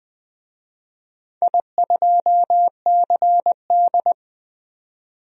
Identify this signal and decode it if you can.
Morse code, heard as a keyed tone, about 20 wpm, 700 Hz. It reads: I2CD
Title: DXCC-100 Call Signs - 4 Characters - 20wpm